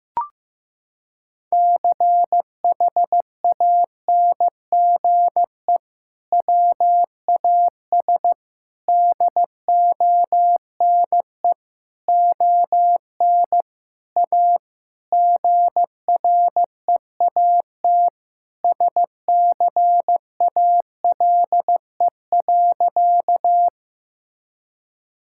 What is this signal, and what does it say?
Morse code, heard as a keyed tone, about 15 wpm, 700 Hz. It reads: CHANGE WAS DONE ON A GREAT SCALE.
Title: Sentences from Top 1000 Words - Review Part 5 - 15wpm